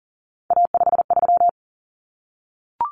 Morse code, heard as a keyed tone, about 40 wpm, 700 Hz. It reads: A53
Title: Common Cut Numbers - 3-Digit 40wpm